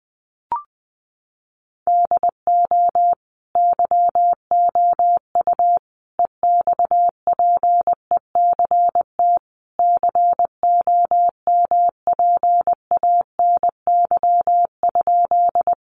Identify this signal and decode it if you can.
Morse code, heard as a keyed tone, about 19 wpm, 700 Hz. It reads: DO YOU EXPECT COMPANY?
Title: Sentences from Top 1100 Words 20wpm